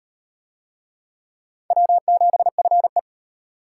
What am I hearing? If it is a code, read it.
Morse code, heard as a keyed tone, about 37 wpm, 700 Hz. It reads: W7FE